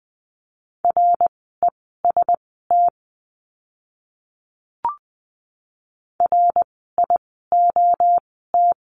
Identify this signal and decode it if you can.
Morse code, heard as a keyed tone, about 20 wpm, spaced as for 15 wpm, 700 Hz. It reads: REST RIOT
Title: 061 Lesson 11f - Letter h - Words 20wpm